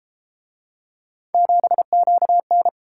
Morse code, heard as a keyed tone, about 31 wpm, 700 Hz. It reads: 7QN